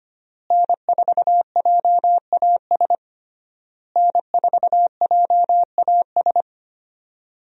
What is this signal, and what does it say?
Morse code, heard as a keyed tone, about 25 wpm, 700 Hz. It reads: N4JAS N4JAS
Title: USA Call Signs - 5 Characters - Repeated 3x - 25wpm